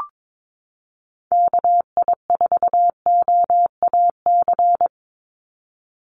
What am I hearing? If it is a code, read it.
Morse code, heard as a keyed tone, about 22 wpm, 700 Hz. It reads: KI4OAC
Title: USA Call Signs 22wpm